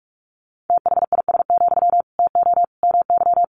Morse code, heard as a keyed tone, about 44 wpm, 700 Hz. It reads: THIS, TO MY
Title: Sentences from Top 1500 Words 45wpm